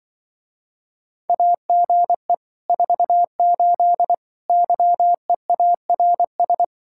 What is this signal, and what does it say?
Morse code, heard as a keyed tone, about 24 wpm, 700 Hz. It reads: AGE 48 YEARS
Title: Speed-Racing Common QSO Phrases - 20wpm